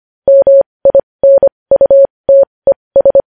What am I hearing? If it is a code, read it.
Morse code, heard as a keyed tone, about 25 wpm, spaced as for 20 wpm, 550 Hz. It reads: MINUTES
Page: News Headlines in Morse Code at 20 WPM